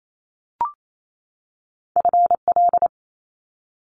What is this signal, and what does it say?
Morse code, heard as a keyed tone, about 28 wpm, 700 Hz. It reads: FL
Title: Top Two Letter Combinations in Words 28wpm